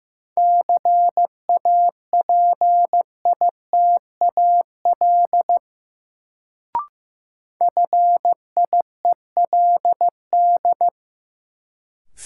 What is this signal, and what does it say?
Morse code, heard as a keyed tone, about 15 wpm, 700 Hz. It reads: CAPITAL FIELD
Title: Sentences from Top 1000 Words - New Word Drill - 15wpm